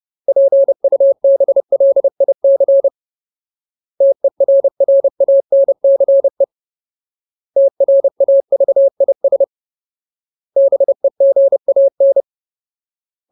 Morse code, heard as a keyed tone, about 30 wpm, 550 Hz. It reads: PUBLIC TERRANCE TRAVIS BEGAN
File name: Ultimate Word List / 30WPM Ultimate